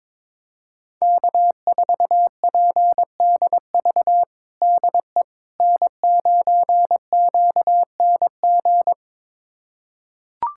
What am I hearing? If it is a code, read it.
Morse code, heard as a keyed tone, about 22 wpm, 700 Hz. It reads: K4PDV DE N9QNG
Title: USA Callsign DE Callsign 22wpm